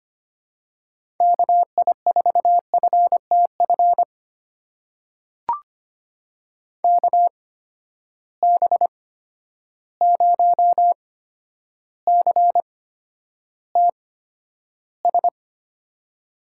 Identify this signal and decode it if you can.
Morse code, heard as a keyed tone, about 25 wpm, spaced as for 11 wpm, 700 Hz. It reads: KI4FTF KB0CTS
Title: USA Call Signs - 6 Characters - 8x Character Spacing - 25wpm